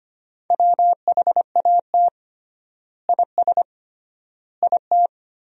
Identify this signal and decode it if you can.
Morse code, heard as a keyed tone, about 25 wpm, 700 Hz. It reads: WHAT IS IT
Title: 3x Word-Spacing - Sentences from Top 100 Words - 25wpm